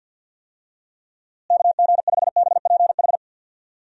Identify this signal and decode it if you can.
Morse code, heard as a keyed tone, about 52 wpm, 700 Hz. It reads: KG5BPH
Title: USA Call Signs - 6 Characters - Repeated 3x - 50wpm